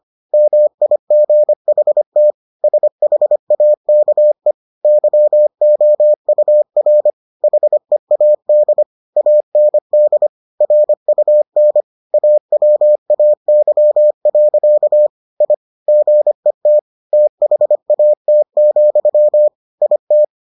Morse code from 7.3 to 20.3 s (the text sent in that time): HEAD AND RUN AWAY. I GET THAT, IT